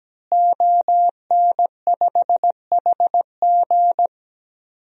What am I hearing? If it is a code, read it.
Morse code, heard as a keyed tone, about 17 wpm, 700 Hz. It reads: ON5HG
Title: DXCC-100 Call Signs - 5 Characters - Repeated 3x - 17wpm